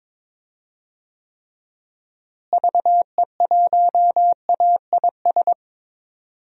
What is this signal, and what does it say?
Morse code, heard as a keyed tone, about 22 wpm, 700 Hz. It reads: VE1AIS
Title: DXCC-100 Call Signs - 6 Characters - Repeated 2x - 22wpm